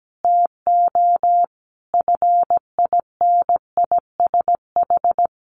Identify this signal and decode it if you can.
Morse code, heard as a keyed tone, about 17 wpm, 700 Hz. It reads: TO FINISH
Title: Sentences from Top 1200 Words 17wpm